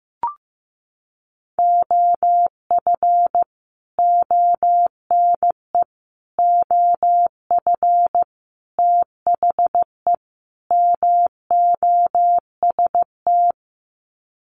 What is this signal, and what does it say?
Morse code, heard as a keyed tone, about 15 wpm, 700 Hz. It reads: OF ONE OF THE MOST
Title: Sets of 5 Words Encore2 15wpm